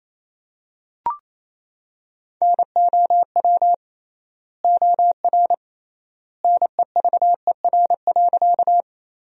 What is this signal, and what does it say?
Morse code, heard as a keyed tone, about 28 wpm, 700 Hz. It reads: NOW OR NEVER.
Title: 3x Word-Spacing - Sentences from Top 300 Words - 28wpm